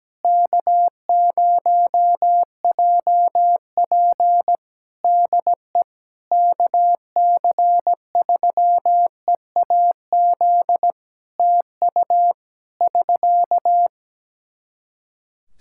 Morse code, heard as a keyed tone, about 17 wpm, 700 Hz. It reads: K0JP DE KC3EAZ TU <SK>
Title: Practice QSO - Exchange 4 - Ending - 17wpm